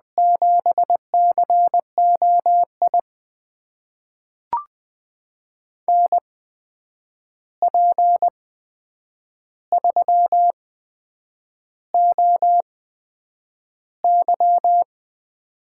Morse code, heard as a keyed tone, about 20 wpm, spaced as for 8 wpm, 700 Hz. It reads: 7COI NP3OY
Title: USA Call Signs - 8x Character Spacing - 20wpm